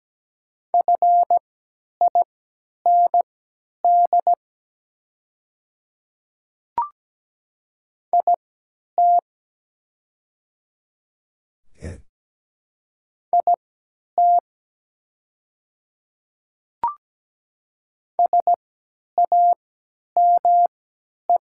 Morse code, heard as a keyed tone, about 17 wpm, spaced as for 10 wpm, 700 Hz. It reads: FIND IT IT SAME
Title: 3x Character Spacing - Top 200 Words - 17wpm